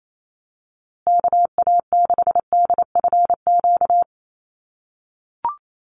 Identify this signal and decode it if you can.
Morse code, heard as a keyed tone, about 28 wpm, 700 Hz. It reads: KA6DFQ
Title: USA Call Signs Repeated 2x 28wpm